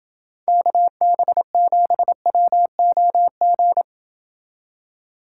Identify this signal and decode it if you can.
Morse code, heard as a keyed tone, about 29 wpm, 700 Hz. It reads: KB7WOG